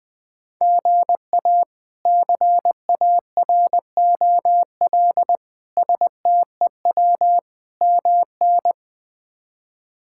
Morse code, heard as a keyed tone, about 21 wpm, 700 Hz. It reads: GA CAROL STEW MN
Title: K1USN SST QSO Practice - 20wpm